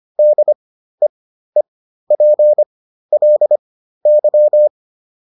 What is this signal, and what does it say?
Morse code, heard as a keyed tone, about 25 wpm, spaced as for 13 wpm, 600 Hz. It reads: DEEPLY